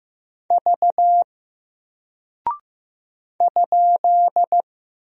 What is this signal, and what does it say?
Morse code, heard as a keyed tone, about 15 wpm, 700 Hz. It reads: V ?
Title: Single Character (Letter:Number:Punctuation:Prosign) Alt 15wpm